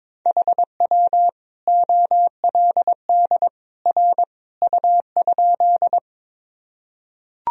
Morse code, heard as a keyed tone, about 22 wpm, 700 Hz. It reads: HW OLD R U?
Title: Sentences from Top 200 Words and Abbreviations 22wpm